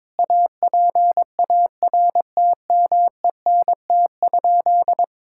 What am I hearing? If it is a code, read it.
Morse code, heard as a keyed tone, about 22 wpm, 700 Hz. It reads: APARTMENT?
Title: Sentences from Top 1000 Words - Review Part 6 - 22wpm